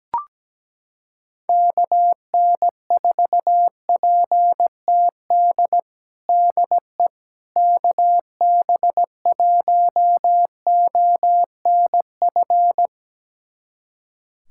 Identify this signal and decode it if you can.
Morse code, heard as a keyed tone, about 17 wpm, 700 Hz. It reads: KN4PTD DE KB1ONF
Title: USA Callsign DE Callsign - 6 Characters - 17wpm